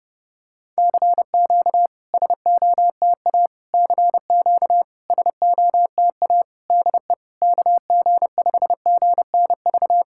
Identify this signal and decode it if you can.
Morse code, heard as a keyed tone, about 32 wpm, 700 Hz. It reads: CQ SOTA CQ SOTA DE KG5GNV
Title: SOTA QSO Practice 30wpm